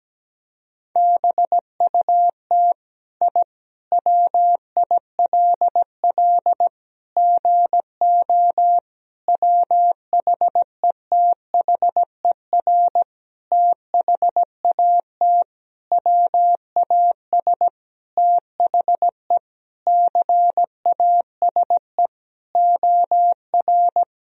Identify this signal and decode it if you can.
Morse code, heard as a keyed tone, about 17 wpm, 700 Hz. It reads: BUT I WILL GO WHETHER THAT WAS THE CASE OR
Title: Sentences from Top 1000 Words - Review Part 3 - 17wpm